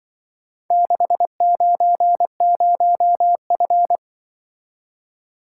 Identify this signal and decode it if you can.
Morse code, heard as a keyed tone, about 24 wpm, 700 Hz. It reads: 690F
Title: Speed-Racing - Four Letter-Number Combinations - 20wpm